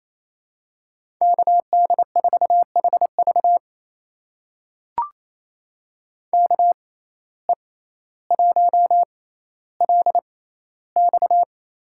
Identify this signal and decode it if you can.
Morse code, heard as a keyed tone, about 28 wpm, spaced as for 14 wpm, 700 Hz. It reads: KD4HV KE1LX
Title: USA Call Signs - 5 Characters - 6x Character Spacing - 28wpm